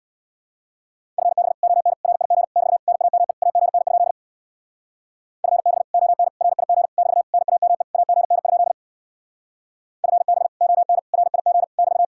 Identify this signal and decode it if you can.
Morse code, heard as a keyed tone, about 66 wpm, 700 Hz. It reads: FB ON RIG <BT> NAME AGN? FB ON RIG <BT> NAME AGN? FB ON RIG <BT>